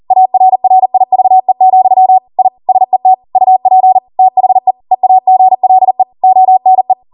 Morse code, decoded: ARRIVE, I SET UP THE EAGLE ONE